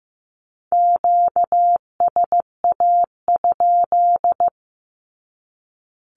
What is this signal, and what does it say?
Morse code, heard as a keyed tone, about 15 wpm, 700 Hz. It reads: QSA?
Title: Q-Codes 15wpm